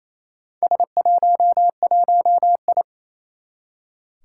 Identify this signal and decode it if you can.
Morse code, heard as a keyed tone, about 28 wpm, 700 Hz. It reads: S11I
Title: Four Challenging Letter-Number Combinations 28wpm